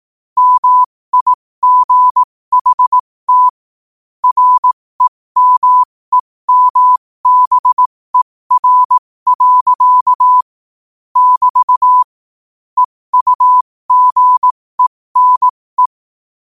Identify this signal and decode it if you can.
Morse code, heard as a keyed tone, about 18 wpm, spaced as for 15 wpm, 1000 Hz. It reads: MIGHT REMEMBER. <BT> EUGENE
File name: qotdfiles / 15wpm